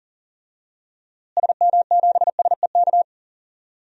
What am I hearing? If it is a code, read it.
Morse code, heard as a keyed tone, about 40 wpm, 700 Hz. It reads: SM7SEK
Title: DXCC-100 Call Signs - 6 Characters - Repeated 3x - 40wpm